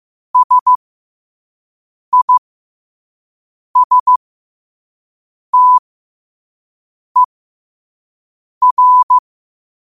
Morse code, heard as a keyed tone, about 15 wpm, spaced as for 5 wpm, 1000 Hz. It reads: SISTER
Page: Quotes for Tue, 12 Aug 2025 in Morse Code at 5 words per minute.